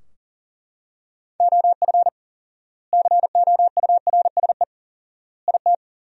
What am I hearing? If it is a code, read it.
Morse code, heard as a keyed tone, about 37 wpm, 700 Hz. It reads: OF COURSE IT
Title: 4x Word-Spacing - Sentences from Top 500 Words - 40wpm